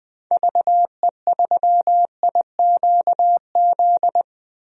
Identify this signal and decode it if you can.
Morse code, heard as a keyed tone, about 20 wpm, 700 Hz. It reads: VE3IQZ
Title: DXCC-100 Call Signs - 6 Characters - Repeated 3x - 20wpm